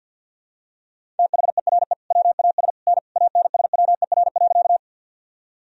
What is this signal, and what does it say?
Morse code, heard as a keyed tone, about 49 wpm, 700 Hz. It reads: THERE WAS N ANSWER.
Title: Sentences from Top 300 Words and Abbreviations 50wpm